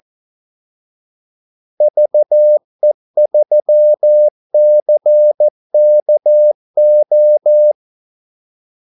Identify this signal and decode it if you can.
Morse code, heard as a keyed tone, about 14 wpm, 600 Hz. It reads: VE3CKO
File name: Callsigns-14wpm